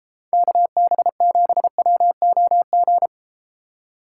Morse code, heard as a keyed tone, about 32 wpm, 700 Hz. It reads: KB7WOG